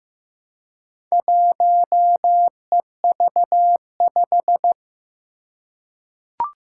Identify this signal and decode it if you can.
Morse code, heard as a keyed tone, about 15 wpm, 700 Hz. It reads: 1EV5
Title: Four Challenging Letter-Number Combinations 15wpm